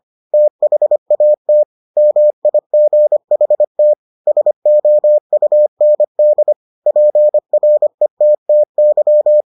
Morse code, heard as a keyed tone, about 25 wpm, 600 Hz. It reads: THAT MIGHT SOUND PRETTY